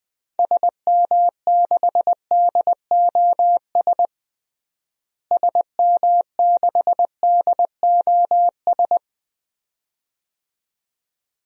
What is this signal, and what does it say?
Morse code, heard as a keyed tone, about 20 wpm, 700 Hz. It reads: SM6DOS SM6DOS
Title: DXCC-100 Call Signs - 6 Characters - Repeated 2x - 20wpm